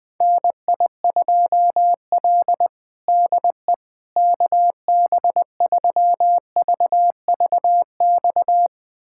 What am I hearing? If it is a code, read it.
Morse code, heard as a keyed tone, about 20 wpm, 700 Hz. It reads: NI2L DE KB3VVX
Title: USA Callsign DE Callsign 20wpm